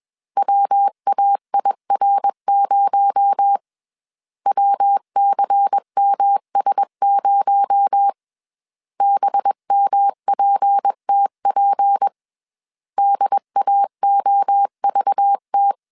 Morse code, heard as a keyed tone, about 21 wpm, 800 Hz. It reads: WAIR0 WCMS0 BMPTP DAOVT
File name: CW / DET / 1 6 1500C S C 20wpm 2116